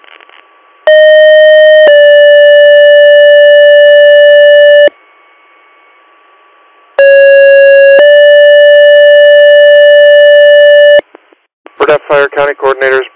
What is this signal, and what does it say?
Two-tone page, heard as a one-second tone, then a three-second tone, about 620 then 580 Hz, about 550 then 580 Hz, with speech. It is heard from 0.9 to 11.0 s.